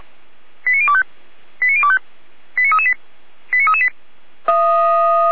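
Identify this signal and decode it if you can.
Five-tone selective call (ZVEI), heard as quick runs of five tones, tones of about 70 ms. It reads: 89025 89025 89338 89338
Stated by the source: ZVEI five-tone selective call received on an SDR.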